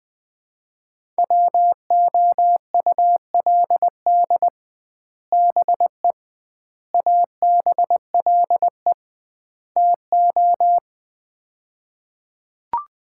Morse code, heard as a keyed tone, about 20 wpm, 700 Hz. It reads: WOULD BE ABLE TO
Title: 2x Word-Spacing - Sets of 4 Words - 20wpm